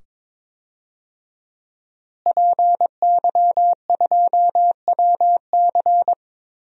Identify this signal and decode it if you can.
Morse code, heard as a keyed tone, about 22 wpm, 700 Hz. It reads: PY2WC
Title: DXCC-100 Call Signs - 5 Characters - Repeated 2x - 22wpm